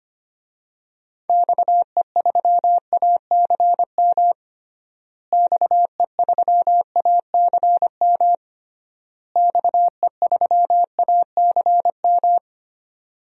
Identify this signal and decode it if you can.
Morse code, heard as a keyed tone, about 25 wpm, 700 Hz. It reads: XE3ACM XE3ACM XE3ACM
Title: DXCC-100 Call Signs - 6 Characters - Repeated 3x - 25wpm